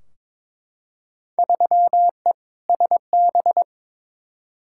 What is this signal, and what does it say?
Morse code, heard as a keyed tone, about 22 wpm, 700 Hz. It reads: 3E SB
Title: Field Day Exchanges Encore - 22wpm